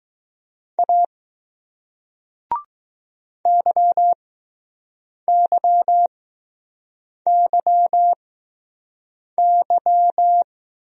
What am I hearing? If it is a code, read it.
Morse code, heard as a keyed tone, about 20 wpm, spaced as for 6 wpm, 700 Hz. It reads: A YYYY